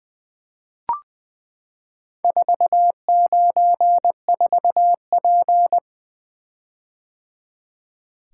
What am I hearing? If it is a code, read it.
Morse code, heard as a keyed tone, about 20 wpm, 700 Hz. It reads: 494P
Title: Four Letter-Number Combinations - 20wpm